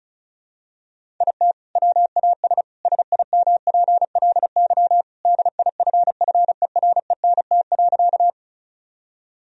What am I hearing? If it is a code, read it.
Morse code, heard as a keyed tone, about 34 wpm, 700 Hz. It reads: IT WAS SIMPLY DIFFERENT.
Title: Sentences from Top 1000 Words - Review Part 4 - 35wpm